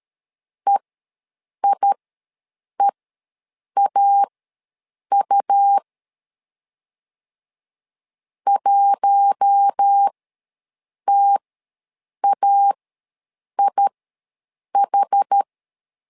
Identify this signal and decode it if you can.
Morse code, heard as a keyed tone, about 13 wpm, spaced as for 7 wpm, 800 Hz. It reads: EIEAU 1TAIH
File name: LEZIONI / 4 Q1 ws 12wpm 2129